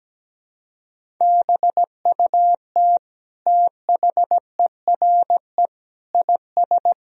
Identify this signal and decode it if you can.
Morse code, heard as a keyed tone, about 17 wpm, 700 Hz. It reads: BUT THERE IS
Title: Sentences from Top 1000 Words - Review Part 4 - 17wpm